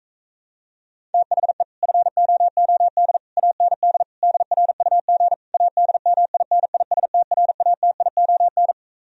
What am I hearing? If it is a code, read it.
Morse code, heard as a keyed tone, about 45 wpm, 700 Hz. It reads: THE FOOD AND DRUG ADMINISTRATION